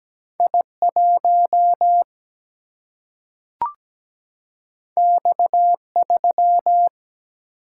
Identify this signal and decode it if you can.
Morse code, heard as a keyed tone, about 17 wpm, 700 Hz. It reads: I1 X3
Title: Single-Letter Single-Number 17wpm